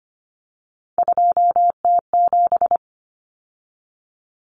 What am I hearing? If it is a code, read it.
Morse code, heard as a keyed tone, about 25 wpm, 700 Hz. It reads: 2T7
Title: Common Cut Numbers - 3-Digit 25wpm